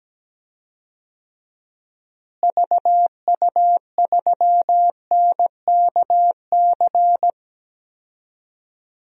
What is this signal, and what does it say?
Morse code, heard as a keyed tone, about 17 wpm, 700 Hz. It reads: VU3NKC
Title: DXCC-100 Call Signs - 6 Characters - Repeated 3x - 17wpm